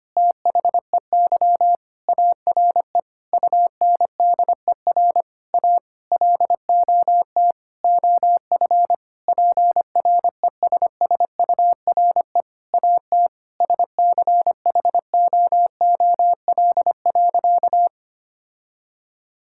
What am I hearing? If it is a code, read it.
Morse code, heard as a keyed tone, about 25 wpm, 700 Hz. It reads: THEY ARE UNDER A LOT OF PRESSURE AT SCHOOL.
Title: Sentences from Top 1000 Words - Review Part 6 - 25wpm